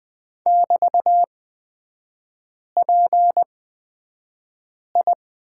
Morse code, heard as a keyed tone, about 20 wpm, spaced as for 5 wpm, 700 Hz. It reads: <BT>PI
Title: 224 Lesson 36e - BT - Three Characters 20wpm